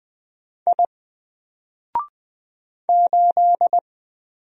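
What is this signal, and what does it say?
Morse code, heard as a keyed tone, about 20 wpm, 700 Hz. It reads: I 8
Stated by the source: Single letter and number Morse drill at 20 wpm, from N4LCD.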